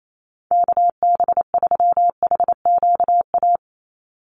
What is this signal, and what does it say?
Morse code, heard as a keyed tone, about 28 wpm, 700 Hz. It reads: KB3HQA
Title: USA Call Signs 28wpm